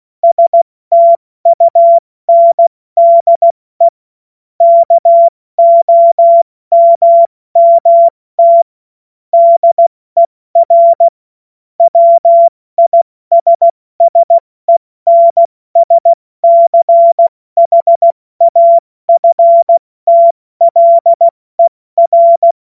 Morse code, heard as a keyed tone, about 16 wpm, 680 Hz. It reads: STUNDE KOMMT DER WISSENSCHAFTLER
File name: Witz-23-September-2025-70-F80